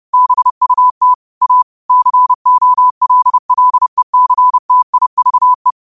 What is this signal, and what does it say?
Morse code, heard as a keyed tone, about 30 wpm, 1000 Hz. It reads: BUT A COLLECTIVE
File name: qotdfiles / 30wpm